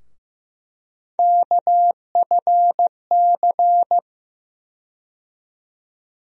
Morse code, heard as a keyed tone, about 15 wpm, 700 Hz. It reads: KFC
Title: USA Call Sign Postfixes - 3 Letters 15wpm